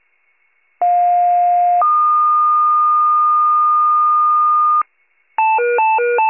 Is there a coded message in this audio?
Yes: two-tone page from 0.8 to 4.8 s, heard as a one-second tone, then a three-second tone, about 710 then 1220 Hz.